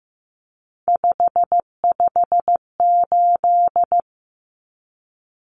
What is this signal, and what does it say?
Morse code, heard as a keyed tone, about 15 wpm, 700 Hz. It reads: 558
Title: Common Cut Numbers - 3-Digit 15wpm